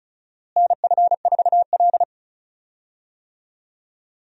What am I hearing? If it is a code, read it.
Morse code, heard as a keyed tone, about 34 wpm, 700 Hz. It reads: NF4L
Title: USA Call Signs - 4 Characters - Repeated 3x - 35wpm